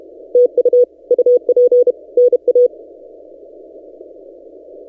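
Morse code, heard as a keyed tone, about 32 wpm, 490 Hz. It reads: TU UP NA
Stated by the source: Morse from DXpedition station A35X on 12 m, posted by N6ML.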